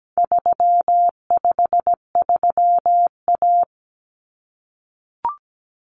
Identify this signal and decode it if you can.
Morse code, heard as a keyed tone, about 17 wpm, 700 Hz. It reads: 353A
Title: Common Cut Numbers - 4-Digit 17wpm